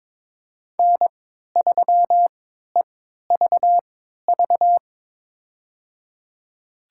Morse code, heard as a keyed tone, about 21 wpm, spaced as for 12 wpm, 700 Hz. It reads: N3EVV
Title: USA Call Signs - 5 Characters - 3x Character Spacing - 22wpm